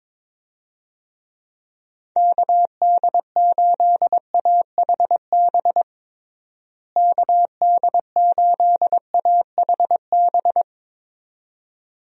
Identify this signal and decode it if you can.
Morse code, heard as a keyed tone, about 21 wpm, 700 Hz. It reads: KD8AHB KD8AHB
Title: USA Call Signs - 6 Characters - Repeated 2x - 22wpm